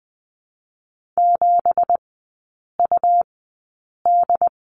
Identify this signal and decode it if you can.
Morse code, heard as a keyed tone, about 20 wpm, 700 Hz. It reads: 7 U D
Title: 218 Lesson 35e - Number 0 - Three Characters 20wpm